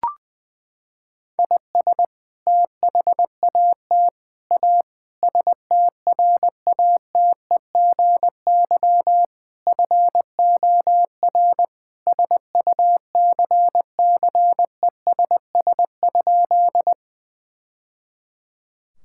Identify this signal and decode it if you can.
Morse code, heard as a keyed tone, about 20 wpm, 700 Hz. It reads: IS THAT A STRATEGY FOR SUCCESS?
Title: Sentences from Top 1000 Words - Review Part 5 - 20wpm